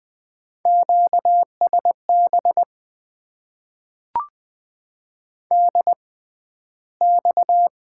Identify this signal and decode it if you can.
Morse code, heard as a keyed tone, about 20 wpm, spaced as for 12 wpm, 700 Hz. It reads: QSB DX
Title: Top 90 QSO Elements - 6x Character Spacing - 20wpm